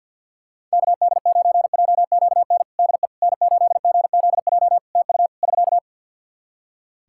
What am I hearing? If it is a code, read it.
Morse code, heard as a keyed tone, about 48 wpm, 700 Hz. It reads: KD9JQN DE N8GZJ TU <SK>
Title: K1USN SST QSO Practice 50wpm